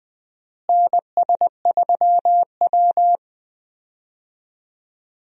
Morse code, heard as a keyed tone, about 20 wpm, 700 Hz. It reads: NS3W
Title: USA Call Signs - 4 Characters - Repeated 3x - 20wpm